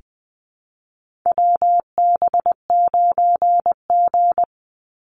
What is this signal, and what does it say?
Morse code, heard as a keyed tone, about 20 wpm, 700 Hz. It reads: WB9G
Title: Call Signs - 20wpm